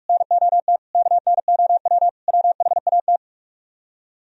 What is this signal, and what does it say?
Morse code, heard as a keyed tone, about 58 wpm, 700 Hz. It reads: NOT KNOW WHAT